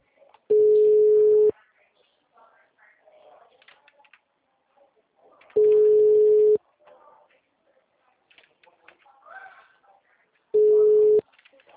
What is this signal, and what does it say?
Ringback tone (Europe, 425 Hz), with speech.